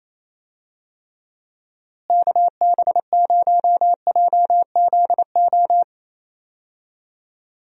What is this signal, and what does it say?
Morse code, heard as a keyed tone, about 28 wpm, 700 Hz. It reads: KB0JZO
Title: USA Call Signs - 6 Characters - Repeated 2x - 28wpm